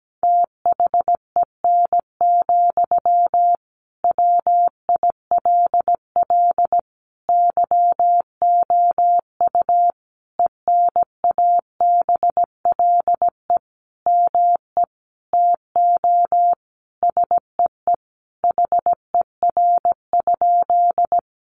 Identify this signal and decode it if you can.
Morse code, heard as a keyed tone, about 17 wpm, 700 Hz. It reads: THEN, WILL YOU ENABLE ME TO SEE HER?
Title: Sentences from Top 1500 Words 17wpm